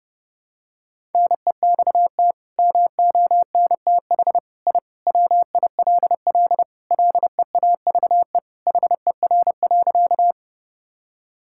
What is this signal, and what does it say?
Morse code, heard as a keyed tone, about 30 wpm, 700 Hz. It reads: NEXT MONTH I WILL LEAVE HER.
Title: Sentences from Top 1000 Words - Review Part 5 - 30wpm